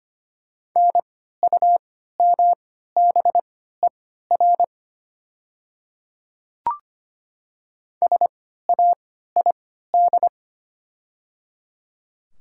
Morse code, heard as a keyed tone, about 25 wpm, spaced as for 14 wpm, 700 Hz. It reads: NUMBER SAID
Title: 3x Character Spacing - Top 100 Words - 25wpm